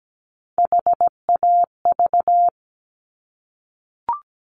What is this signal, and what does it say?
Morse code, heard as a keyed tone, about 17 wpm, 700 Hz. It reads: HAV
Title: Top Three Letter Combinations in Words 17wpm